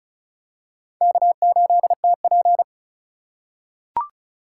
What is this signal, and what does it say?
Morse code, heard as a keyed tone, about 36 wpm, 700 Hz. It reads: K8TP
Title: USA Call Signs - 4 Characters - 3x Character Spacing - 35wpm